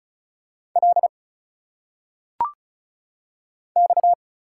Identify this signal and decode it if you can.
Morse code, heard as a keyed tone, about 35 wpm, 700 Hz. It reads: L X
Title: Single Letters - Phonetic Pronunciation 35wpm